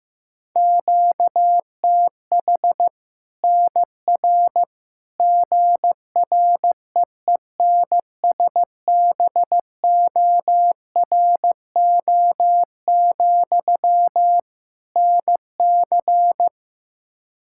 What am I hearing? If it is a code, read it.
Morse code, heard as a keyed tone, about 15 wpm, 700 Hz. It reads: QTH NR GREENSBORO, NC
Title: Practice QSO - Exchange 1 - Beginning - 15wpm